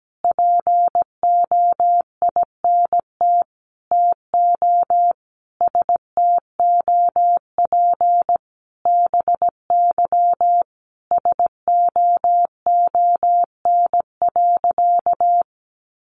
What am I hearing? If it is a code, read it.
Morse code, heard as a keyed tone, about 17 wpm, 700 Hz. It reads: POINT TO STOP BY SOON.
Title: Sentences from Top 1500 Words 17wpm